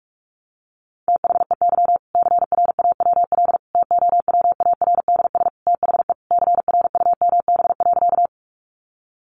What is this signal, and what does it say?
Morse code, heard as a keyed tone, about 44 wpm, 700 Hz. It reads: THEY CRAWL TOWARDS THE CRUMB.
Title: Additional Easy Phrases 45wpm